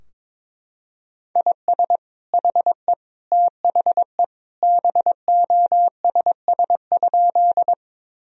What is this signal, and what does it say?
Morse code, heard as a keyed tone, about 22 wpm, 700 Hz. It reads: IS HE THE BOSS?
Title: Sentences from Top 1000 Words - Review Part 4 - 22wpm